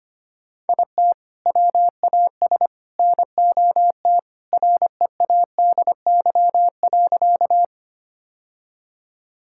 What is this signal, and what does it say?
Morse code, heard as a keyed tone, about 25 wpm, 700 Hz. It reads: IT WAS NOT READY.
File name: practice / 25wpm / Sentences with 2-Word Phrases - 25wpm